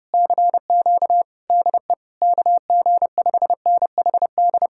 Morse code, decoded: CQ DE KG5NHD

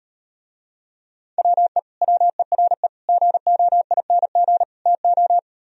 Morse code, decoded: WE WERE GOING TO

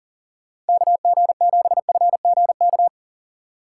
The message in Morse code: KG7FGK

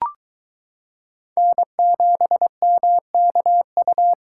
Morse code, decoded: N7MKU

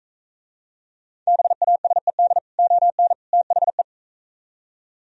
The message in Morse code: BASED ON THE

